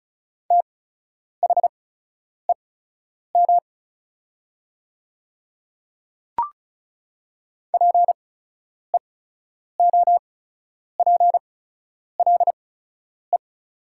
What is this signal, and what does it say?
Morse code, heard as a keyed tone, about 36 wpm, spaced as for 10 wpm, 700 Hz. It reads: THEM PEOPLE
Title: 8x Character Spacing - Top 300 Words - 35wpm